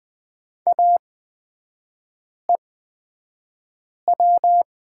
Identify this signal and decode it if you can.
Morse code, heard as a keyed tone, about 20 wpm, spaced as for 5 wpm, 690 Hz. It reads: AEW
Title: 236 Lesson 38e - Letter q - Three Characters 20wpm